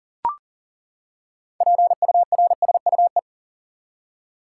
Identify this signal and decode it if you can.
Morse code, heard as a keyed tone, about 40 wpm, 700 Hz. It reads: PURSUE
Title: Six Letter Words - 40wpm